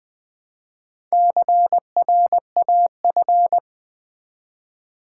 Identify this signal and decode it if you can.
Morse code, heard as a keyed tone, about 20 wpm, 700 Hz. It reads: CRAF
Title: Build A Word - 1 Letter At A Time - 3 to 5-Letter Words - 20wpm